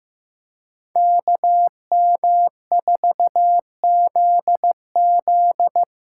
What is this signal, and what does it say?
Morse code, heard as a keyed tone, about 15 wpm, 700 Hz. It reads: KM4ZZ
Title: K1USN SST QSO Practice - 15wpm